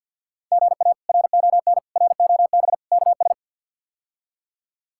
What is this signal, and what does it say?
Morse code, heard as a keyed tone, about 51 wpm, 700 Hz. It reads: GA RON ROB KS
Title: K1USN SST QSO Practice - 50wpm